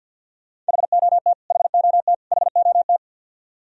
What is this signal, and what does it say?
Morse code, heard as a keyed tone, about 51 wpm, 700 Hz. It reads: HOT HOT HOT
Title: Top 100 Words Repeated 50wpm